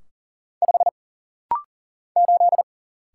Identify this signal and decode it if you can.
Morse code, heard as a keyed tone, about 39 wpm, 700 Hz. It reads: 5 8
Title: Single Letter-Number - Rapid-Fire 40wpm